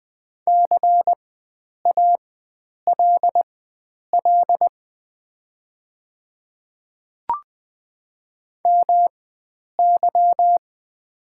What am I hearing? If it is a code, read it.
Morse code, heard as a keyed tone, about 20 wpm, spaced as for 9 wpm, 700 Hz. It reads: CALL MY
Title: 4x Character Spacing - Top 100 Words - 20wpm